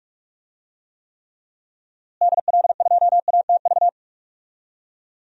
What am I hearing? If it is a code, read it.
Morse code, heard as a keyed tone, about 42 wpm, 700 Hz. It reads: DL2ATV
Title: DXCC-100 Call Signs - 6 Characters - Repeated 3x - 45wpm